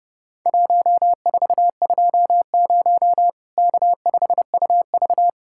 Morse code, heard as a keyed tone, about 32 wpm, 700 Hz. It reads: 1420 K5UV
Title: Worked All Europe DX Contest - QTC Exchange - 30wpm